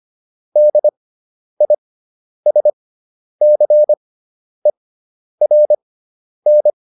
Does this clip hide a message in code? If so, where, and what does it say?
Yes: Morse code from 0.6 to 6.7 s, heard as a keyed tone, about 25 wpm, spaced as for 10 wpm, 600 Hz. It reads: DISCERN